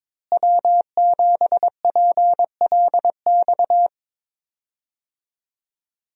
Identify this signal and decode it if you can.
Morse code, heard as a keyed tone, about 22 wpm, 700 Hz. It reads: W7PLX
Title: USA Call Signs - 5 Characters - Repeated 3x - 22wpm